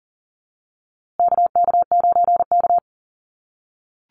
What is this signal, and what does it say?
Morse code, heard as a keyed tone, about 38 wpm, 700 Hz. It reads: KK9K
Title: USA Call Signs Repeated 3x 40wpm